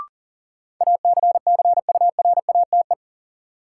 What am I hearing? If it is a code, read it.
Morse code, heard as a keyed tone, about 40 wpm, 700 Hz. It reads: ACCURATE
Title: Sentences from Top 1200 Words - New Word Drill Alt - 40wpm